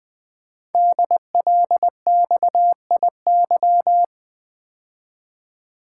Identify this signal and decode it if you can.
Morse code, heard as a keyed tone, about 20 wpm, 700 Hz. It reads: DLXIY
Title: Code Groups - 5 Letters - 20wpm